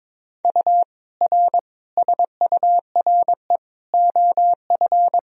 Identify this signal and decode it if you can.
Morse code, heard as a keyed tone, about 22 wpm, 700 Hz. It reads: U R SURE OF
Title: Sentences from Top 300 Words and Abbreviations 22wpm